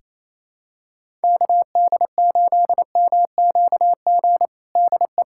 Morse code, heard as a keyed tone, about 28 wpm, 700 Hz. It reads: KD8MQG DE
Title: USA Callsign DE Callsign 28wpm